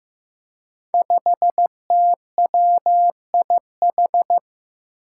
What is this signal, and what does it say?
Morse code, heard as a keyed tone, about 15 wpm, 700 Hz. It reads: 5TWIH
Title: Five Challenging Letter-Number Combinations 15wpm